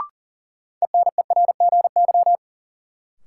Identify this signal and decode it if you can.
Morse code, heard as a keyed tone, about 40 wpm, 700 Hz. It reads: ENERGY
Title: Sentences from Top 900 Words - New Word Drill - 40wpm